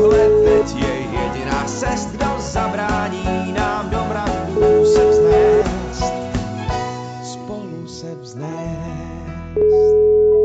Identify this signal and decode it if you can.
Ringback tone (Europe, 425 Hz), with noise.